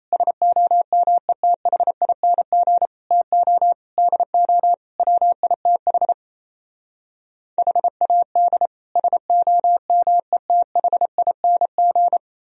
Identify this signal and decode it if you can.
Morse code, heard as a keyed tone, about 37 wpm, 700 Hz. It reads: SOMETHING TO DO WITH HAD SOMETHING